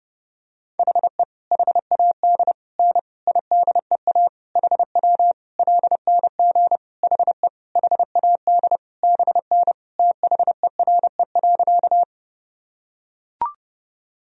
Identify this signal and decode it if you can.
Morse code, heard as a keyed tone, about 30 wpm, 700 Hz. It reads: HE HAD N IDEA HW LNG HE HAD BN THERE.
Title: Sentences from Top 300 Words and Abbreviations 30wpm